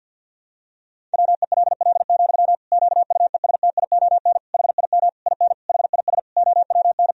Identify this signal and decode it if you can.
Morse code, heard as a keyed tone, about 52 wpm, 700 Hz. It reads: WELL, QUESTION HIM IN HIS OWN
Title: Sentences from Top 1000 Words - Review Part 3 - 50wpm